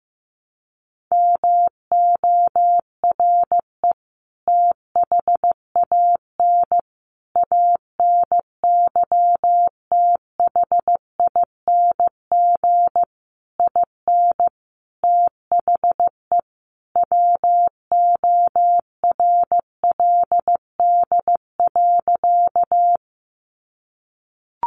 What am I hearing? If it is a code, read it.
Morse code, heard as a keyed tone, about 15 wpm, 700 Hz. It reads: MORE THAN ANYTHING IN THE WORLD.
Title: Sentences from Top 1000 Words 15wpm